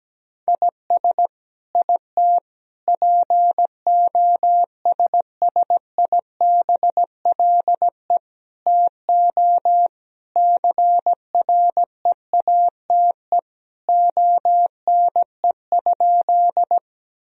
Morse code, heard as a keyed tone, about 17 wpm, 700 Hz. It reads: IS IT POSSIBLE TO CREATE ONE?
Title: Sentences from Top 1000 Words - Review Part 3 - 17wpm